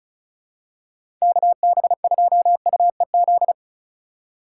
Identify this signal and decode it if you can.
Morse code, heard as a keyed tone, about 36 wpm, 700 Hz. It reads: KB2UEZ
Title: USA Call Signs - 6 Characters - 2x Character Spacing - 35wpm